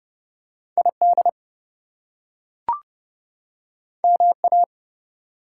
Morse code, heard as a keyed tone, about 32 wpm, 700 Hz. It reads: ID MA
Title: US States and Canadian Provinces 30wpm